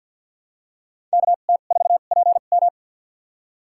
Morse code, heard as a keyed tone, about 52 wpm, spaced as for 37 wpm, 700 Hz. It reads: KT4PM
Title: USA Call Signs - 2x Character Spacing - 50wpm